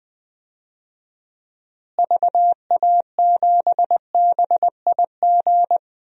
Morse code, heard as a keyed tone, about 20 wpm, 700 Hz. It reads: VA7BIG
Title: DXCC-100 Call Signs - 6 Characters - Repeated 3x - 20wpm